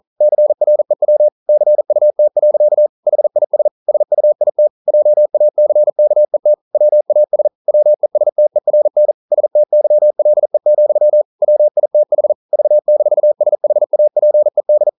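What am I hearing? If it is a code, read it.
Morse code, heard as a keyed tone, about 41 wpm, 600 Hz. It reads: CREW CUT. HIS SUIT JACKET WAS WESTERN STYLE, WITH V-SHAPED
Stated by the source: Morse code practice at 41 wpm, from CW med Gnister.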